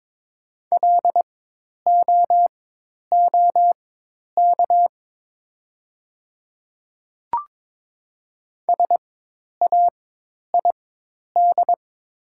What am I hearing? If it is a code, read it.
Morse code, heard as a keyed tone, about 22 wpm, spaced as for 10 wpm, 700 Hz. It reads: LOOK SAID
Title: 4x Character Spacing - Top 300 Words - 22wpm